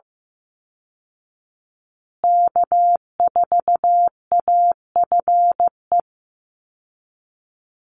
Morse code, heard as a keyed tone, about 15 wpm, 700 Hz. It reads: K4AFE
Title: Call Signs Repeated 2x - 15wpm